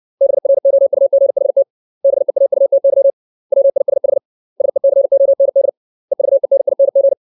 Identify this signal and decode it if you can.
Morse code, heard as a keyed tone, about 60 wpm, 550 Hz. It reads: BROUGHT BEAUTY WISH SECOND EVENING
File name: 632 words / 632 Words 60WPM